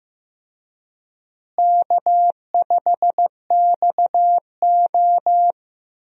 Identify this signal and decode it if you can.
Morse code, heard as a keyed tone, about 16 wpm, 700 Hz. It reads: K5XO
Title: USA Call Signs - 4 Characters - Repeated 3x - 15wpm